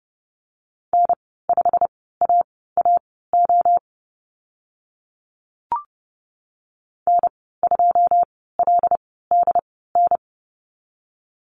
Morse code, heard as a keyed tone, about 30 wpm, spaced as for 17 wpm, 700 Hz. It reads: N5AAO N2LDN
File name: Lesson 15g - Number 5 - Call Signs 30wpm